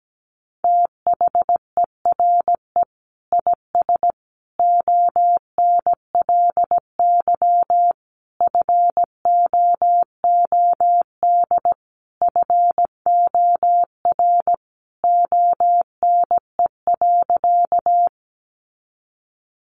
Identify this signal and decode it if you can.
Morse code, heard as a keyed tone, about 17 wpm, 700 Hz. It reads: THERE IS ONLY FOOD FOR ONE.
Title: Sentences from Top 300 Words 17wpm